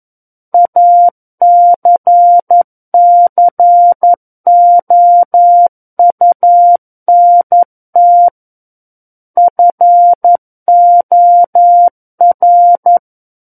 Morse code, heard as a keyed tone, about 11 wpm, 700 Hz. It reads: ACCOUNT FOR
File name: 11 WPM